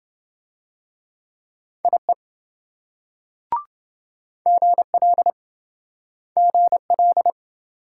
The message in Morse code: IE GL GL

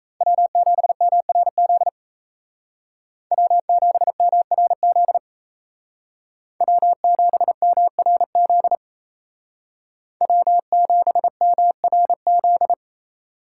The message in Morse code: W7MRZ W7MRZ W7MRZ W7MRZ

